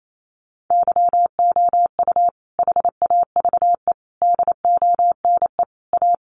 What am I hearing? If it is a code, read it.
Morse code, heard as a keyed tone, about 28 wpm, 700 Hz. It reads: YOU HAVE DONE A
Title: Sentences from Top 700 Words 28wpm